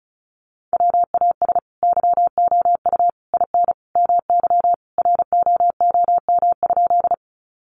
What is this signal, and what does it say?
Morse code, heard as a keyed tone, about 34 wpm, 700 Hz. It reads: WAS YOU IN MY ROOM?
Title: Sentences from Top 300 Words 35wpm